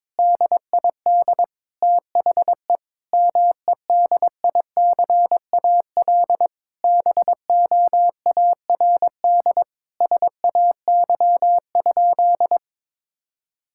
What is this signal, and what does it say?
Morse code, heard as a keyed tone, about 22 wpm, 700 Hz. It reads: DID THE MEDICAL BOARD SAY?
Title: Sentences from Top 1000 Words - Review Part 5 - 22wpm